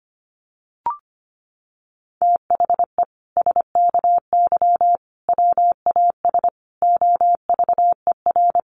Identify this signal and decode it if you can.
Morse code, heard as a keyed tone, about 25 wpm, 700 Hz. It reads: THE SKY WAS OVER
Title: Sentences from Top 1200 Words 25wpm